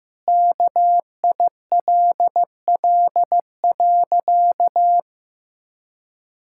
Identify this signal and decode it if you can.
Morse code, heard as a keyed tone, about 15 wpm, 700 Hz. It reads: KILL.
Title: Sentences from Top 1600 Words 15wpm